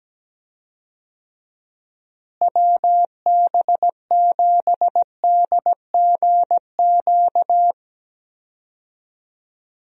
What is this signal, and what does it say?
Morse code, heard as a keyed tone, about 17 wpm, 700 Hz. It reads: WB7DGQ
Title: USA Call Signs - 6 Characters - Repeated 2x - 17wpm